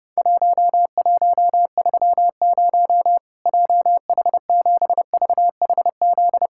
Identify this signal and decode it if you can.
Morse code, heard as a keyed tone, about 30 wpm, 700 Hz. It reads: 1130 JH7VHZ
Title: Worked All Europe DX Contest - QTC Exchange - 30wpm